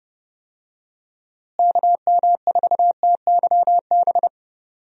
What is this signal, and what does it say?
Morse code, heard as a keyed tone, about 28 wpm, 700 Hz. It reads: KM4TYB
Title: USA Call Signs - 6 Characters - Repeated 2x - 30wpm